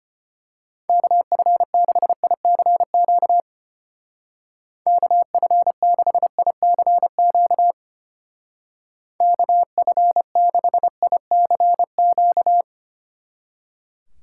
Morse code, heard as a keyed tone, about 28 wpm, 700 Hz. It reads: KF6ICQ KF6ICQ KF6ICQ
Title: Speed Racing - USA Call Signs 25wpm